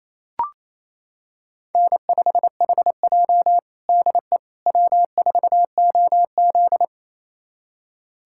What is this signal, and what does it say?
Morse code, heard as a keyed tone, about 28 wpm, 700 Hz. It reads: N5HJ DE W4OZ
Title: USA Callsign DE Callsign - 4 Characters - 28wpm